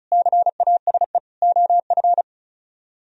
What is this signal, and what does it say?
Morse code, heard as a keyed tone, about 44 wpm, 700 Hz. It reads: CASE OF